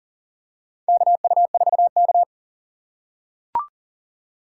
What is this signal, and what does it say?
Morse code, heard as a keyed tone, about 40 wpm, 700 Hz. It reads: KU4K
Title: USA Call Signs - 4 Characters - 3x Character Spacing - 40wpm